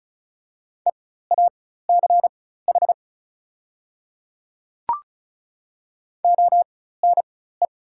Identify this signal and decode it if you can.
Morse code, heard as a keyed tone, about 36 wpm, spaced as for 16 wpm, 700 Hz. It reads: EACH ONE
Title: 4x Character Spacing - Top 300 Words - 35wpm